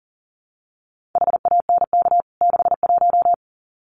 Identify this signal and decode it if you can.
Morse code, heard as a keyed tone, about 40 wpm, 700 Hz. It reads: HANK 61
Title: CWT Exchange 40wpm